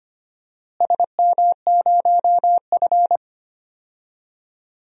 Morse code, heard as a keyed tone, about 25 wpm, 700 Hz. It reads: SM0F
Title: DXCC-100 Call Signs - 4 Characters - Repeated 2x - 25wpm